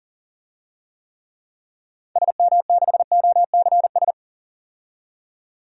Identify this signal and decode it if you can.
Morse code, heard as a keyed tone, about 40 wpm, 700 Hz. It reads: SM6OCS
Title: DXCC-100 Call Signs - 6 Characters - 40wpm